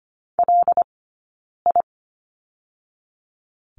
Morse code, heard as a keyed tone, about 25 wpm, 700 Hz. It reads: L I
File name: Lesson 17d - Letter u - Two Characters 25wpm